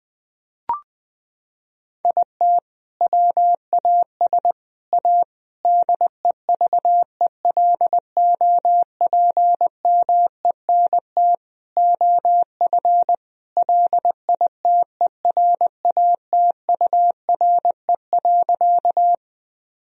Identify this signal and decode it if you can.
Morse code, heard as a keyed tone, about 20 wpm, 700 Hz. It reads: IT WAS A DEVELOPMENT OF LITERATURE.
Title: Sentences from Top 1000 Words - Review Part 4 - 20wpm